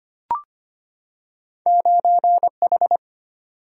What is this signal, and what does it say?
Morse code, heard as a keyed tone, about 25 wpm, 700 Hz. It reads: 9H
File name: Two Letter-Number Combinations - 25wpm